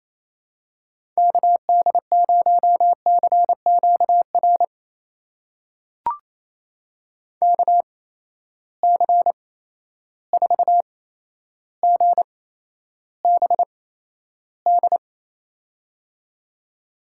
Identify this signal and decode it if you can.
Morse code, heard as a keyed tone, about 28 wpm, spaced as for 12 wpm, 700 Hz. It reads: KD0CQR KC4GBD
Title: USA Call Signs - 6 Characters - 8x Character Spacing - 28wpm